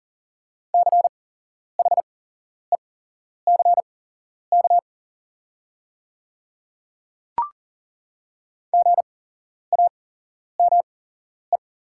Morse code, heard as a keyed tone, about 41 wpm, spaced as for 11 wpm, 690 Hz. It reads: CHECK GAME
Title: 8x Character Spacing - Top 500 Words - 40wpm